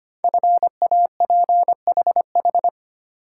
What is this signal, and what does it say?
Morse code, heard as a keyed tone, about 25 wpm, 700 Hz. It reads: FAPHH
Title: Code Groups - 5 Letters - 25wpm